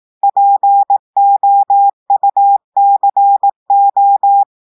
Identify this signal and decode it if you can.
Morse code, heard as a keyed tone, about 18 wpm, 800 Hz. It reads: POUCO
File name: audios / portugais / portugais texte en clair 1 vitesse 18 mots minutes